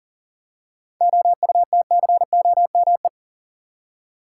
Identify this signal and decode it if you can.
Morse code, heard as a keyed tone, about 40 wpm, 700 Hz. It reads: OUTCOME
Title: Seven Letter Words - 40wpm